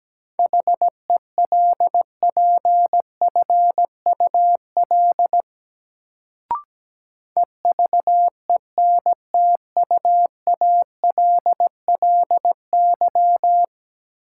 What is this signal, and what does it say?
Morse code, heard as a keyed tone, about 17 wpm, 700 Hz. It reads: HELPFUL EVENTUALLY
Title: Sentences from Top 1000 Words - Review Part 6 - New Word Drill - 17wpm